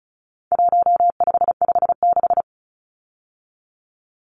Morse code, heard as a keyed tone, about 34 wpm, 700 Hz. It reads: 1556
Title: Numbers - 4-Digit 35wpm